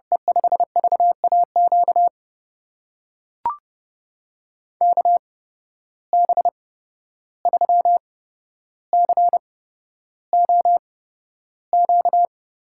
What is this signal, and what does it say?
Morse code, heard as a keyed tone, about 30 wpm, spaced as for 12 wpm, 700 Hz. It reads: E5VAQ KB3COQ
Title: USA Call Signs - 6 Characters - 8x Character Spacing - 30wpm